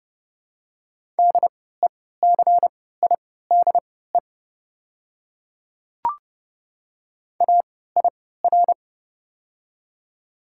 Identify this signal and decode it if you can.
Morse code, heard as a keyed tone, about 30 wpm, spaced as for 17 wpm, 700 Hz. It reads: DECIDE AIR
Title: 3x Character Spacing - Top 500 Words - 30wpm